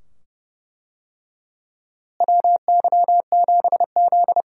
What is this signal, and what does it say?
Morse code, heard as a keyed tone, about 32 wpm, 700 Hz. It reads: WY7Z
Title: USA Call Signs - 4 Characters - Repeated 2x - 30wpm